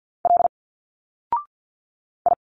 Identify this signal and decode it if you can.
Morse code, heard as a keyed tone, about 46 wpm, 700 Hz. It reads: L I
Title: Single Letters 50wpm